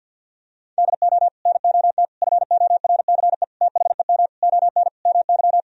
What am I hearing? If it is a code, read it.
Morse code, heard as a keyed tone, about 47 wpm, 700 Hz. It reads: DO NOT FORCE THEM ON MY
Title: Sentences from Top 1000 Words - Review Part 3 - 50wpm